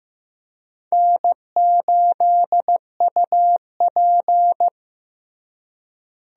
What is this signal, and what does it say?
Morse code, heard as a keyed tone, about 15 wpm, 700 Hz. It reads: N8UP
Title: USA Call Signs - 4 Characters - Repeated 3x - 15wpm